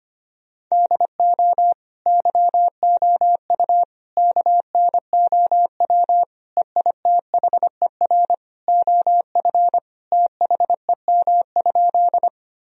Morse code, heard as a keyed tone, about 25 wpm, 690 Hz. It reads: DO YOU KNOW EITHER OF THEM?
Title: Sentences from Top 1000 Words - Review Part 4 - 25wpm